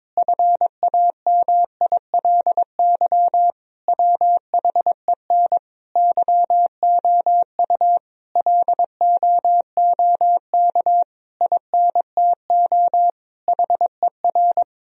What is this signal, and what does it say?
Morse code, heard as a keyed tone, about 22 wpm, 700 Hz. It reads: FAMILY WHEN YOU LOOK INTO HER
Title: Sentences from Top 1000 Words - Review Part 4 - 22wpm